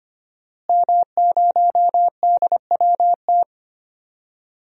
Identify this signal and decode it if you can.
Morse code, heard as a keyed tone, about 25 wpm, 700 Hz. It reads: M0DWT
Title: DXCC-100 Call Signs - 5 Characters - Repeated 3x - 25wpm